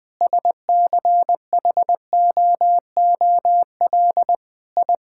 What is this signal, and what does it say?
Morse code, heard as a keyed tone, about 20 wpm, 700 Hz. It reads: SCHOOL I
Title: Sentences with 6-Word Phrases - 20wpm